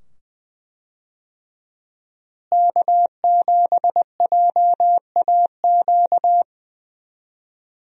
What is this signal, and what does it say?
Morse code, heard as a keyed tone, about 20 wpm, 700 Hz. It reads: K7JAQ
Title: USA Call Signs - 5 Characters - Repeated 2x - 20wpm